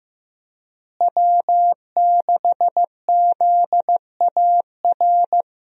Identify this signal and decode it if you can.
Morse code, heard as a keyed tone, about 15 wpm, 700 Hz. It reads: W6ZAR
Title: USA Call Signs - 5 Characters - Repeated 2x - 15wpm